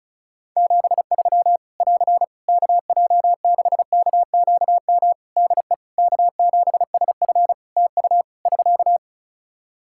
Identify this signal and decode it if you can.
Morse code, heard as a keyed tone, about 36 wpm, 700 Hz. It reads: 73 <AR> KJ6KQM DE K7SF TU <SK>
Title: Practice QSO - Exchange 4 - Ending - 35wpm